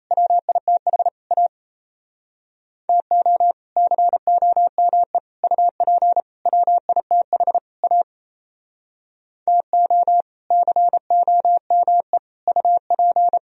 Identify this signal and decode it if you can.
Morse code, heard as a keyed tone, about 38 wpm, 700 Hz. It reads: WITH A TO COME UP WITH A TO COME UP